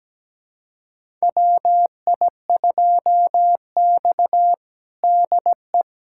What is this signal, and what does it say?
Morse code, heard as a keyed tone, about 17 wpm, 700 Hz. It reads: WI2X DE
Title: USA Callsign DE Callsign - 4 Characters - 17wpm